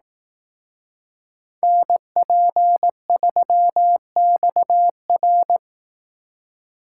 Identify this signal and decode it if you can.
Morse code, heard as a keyed tone, about 18 wpm, 700 Hz. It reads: NP3XR
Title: Speed Racing - USA Call Signs 15wpm